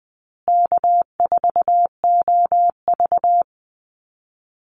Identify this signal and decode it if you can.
Morse code, heard as a keyed tone, about 20 wpm, 700 Hz. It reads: K4OV
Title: Easy Call Signs Repeated 3x - 20wpm